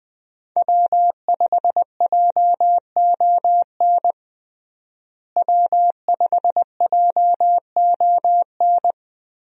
Morse code, heard as a keyed tone, about 20 wpm, 700 Hz. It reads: W5JON W5JON
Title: USA Call Signs - 5 Characters - Repeated 2x - 20wpm